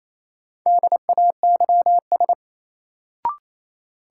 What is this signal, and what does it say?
Morse code, heard as a keyed tone, about 28 wpm, 700 Hz. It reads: DAYS
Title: Sentences from Top 1000 Words - Review Part 5 - New Word Drill - 28wpm